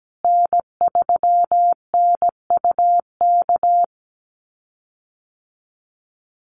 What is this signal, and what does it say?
Morse code, heard as a keyed tone, about 17 wpm, 700 Hz. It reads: N3NUK
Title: USA Call Signs Repeated 2x 17wpm